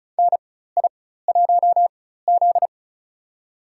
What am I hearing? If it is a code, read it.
Morse code, heard as a keyed tone, about 36 wpm, spaced as for 16 wpm, 700 Hz. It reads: NI1Z
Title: USA Call Signs - 4x Character Spacing - 35wpm